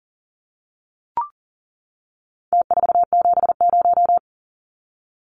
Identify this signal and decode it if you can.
Morse code, heard as a keyed tone, about 40 wpm, 700 Hz. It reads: T470
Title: Numbers - Contest Style - 4-Digit 40wpm